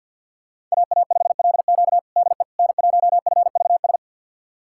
Morse code, heard as a keyed tone, about 47 wpm, 700 Hz. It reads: AA5LQ DE N1FVS
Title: USA Callsign DE Callsign - 5 Characters - 50wpm